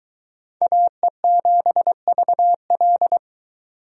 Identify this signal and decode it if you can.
Morse code, heard as a keyed tone, about 22 wpm, 700 Hz. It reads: AE7VL